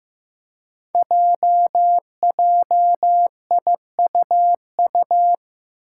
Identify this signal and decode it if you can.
Morse code, heard as a keyed tone, about 15 wpm, 700 Hz. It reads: JJIUU
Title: Five Challenging Letter-Number Combinations 15wpm